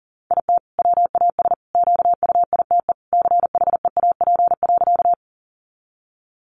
Morse code, heard as a keyed tone, about 40 wpm, 700 Hz. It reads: IT WAS QUITE CHEAP.
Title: Sentences from Top 1200 Words 40wpm